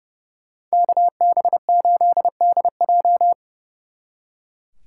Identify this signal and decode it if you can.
Morse code, heard as a keyed tone, about 36 wpm, 700 Hz. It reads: KB8DJ
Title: Speed Racing - USA 2x2 Call Signs 30wpm